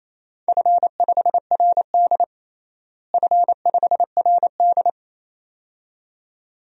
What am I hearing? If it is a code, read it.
Morse code, heard as a keyed tone, about 28 wpm, 700 Hz. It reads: F5RD F5RD
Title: DXCC-100 Call Signs - 4 Characters - Repeated 2x - 28wpm